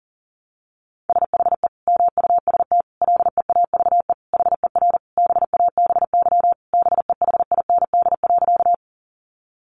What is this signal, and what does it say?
Morse code, heard as a keyed tone, about 40 wpm, 700 Hz. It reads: SHE MUST LEAVE HER BABY BEHIND.
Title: Sentences from Top 1300 Words 40wpm